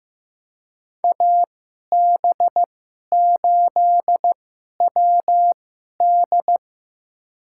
Morse code, heard as a keyed tone, about 15 wpm, spaced as for 11 wpm, 700 Hz. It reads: AB8WD
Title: USA Call Signs - 5 Characters - 2x Character Spacing - 15wpm